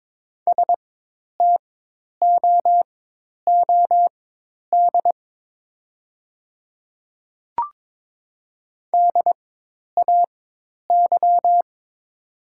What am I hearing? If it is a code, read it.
Morse code, heard as a keyed tone, about 22 wpm, spaced as for 10 wpm, 700 Hz. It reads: STOOD DAY
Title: 4x Character Spacing - Top 500 Words - 22wpm